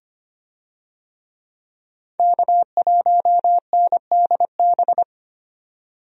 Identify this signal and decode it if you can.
Morse code, heard as a keyed tone, about 25 wpm, 700 Hz. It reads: K1NDB
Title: USA Call Signs - 5 Characters - 25wpm